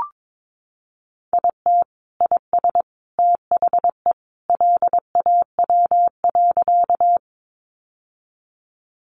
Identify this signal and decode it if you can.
Morse code, heard as a keyed tone, about 22 wpm, 700 Hz. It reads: IT IS THE LAW.
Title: Sentences from Top 600 Words 22wpm